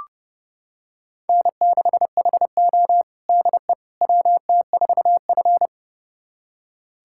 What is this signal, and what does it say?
Morse code, heard as a keyed tone, about 30 wpm, 700 Hz. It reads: N6HO DE WT4F
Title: USA Callsign DE Callsign - 4 Characters - 30wpm